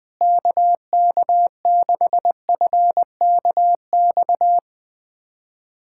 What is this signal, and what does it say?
Morse code, heard as a keyed tone, about 20 wpm, 700 Hz. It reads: KK6FKX
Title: USA Callsign DE Callsign - 6 Characters - 20wpm